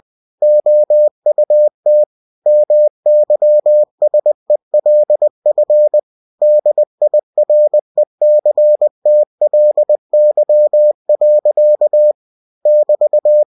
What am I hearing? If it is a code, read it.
Morse code, heard as a keyed tone, about 20 wpm, 600 Hz. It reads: OUT MYSELF DIRECTLY. <BT>
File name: cw-books / winnie-the-pooh / winnie-20wpm-page-0016